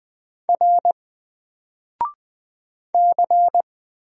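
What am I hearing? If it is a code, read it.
Morse code, heard as a keyed tone, about 20 wpm, 700 Hz. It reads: R C
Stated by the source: Single letter and number Morse drill at 20 wpm, from N4LCD.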